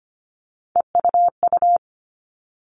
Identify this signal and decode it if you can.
Morse code, heard as a keyed tone, about 25 wpm, 700 Hz. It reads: EUU
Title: All Cut Numbers - 3-Digit 25wpm